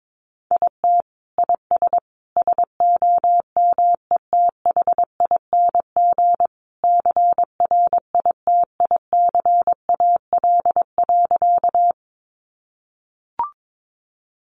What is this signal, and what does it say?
Morse code, heard as a keyed tone, about 22 wpm, 700 Hz. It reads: IT IS SOMETHING CRITICAL.
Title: Sentences from Top 1100 Words 22wpm